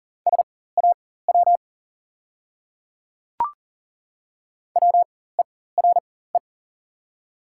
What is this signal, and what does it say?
Morse code, heard as a keyed tone, about 40 wpm, spaced as for 19 wpm, 690 Hz. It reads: SAW WERE
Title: 4x Character Spacing - Top 300 Words - 40wpm